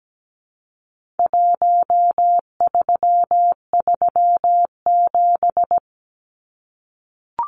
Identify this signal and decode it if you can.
Morse code, heard as a keyed tone, about 17 wpm, 700 Hz. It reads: 1337
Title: Numbers - 4-Digit 17wpm